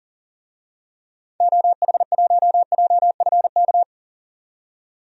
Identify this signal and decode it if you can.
Morse code, heard as a keyed tone, about 37 wpm, 700 Hz. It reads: OH1JFK
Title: DXCC-100 Call Signs - 6 Characters - 40wpm